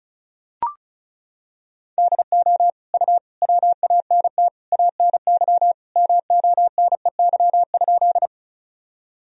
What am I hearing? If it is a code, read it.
Morse code, heard as a keyed tone, about 36 wpm, 700 Hz. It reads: DO U WANT ANY MONEY?
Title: Sentences from Top 400 Words and Abbreviations 35wpm